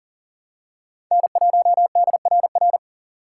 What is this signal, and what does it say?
Morse code, heard as a keyed tone, about 40 wpm, 700 Hz. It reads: N1DRR
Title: USA Call Signs - 5 Characters - 4x Character Spacing - 40wpm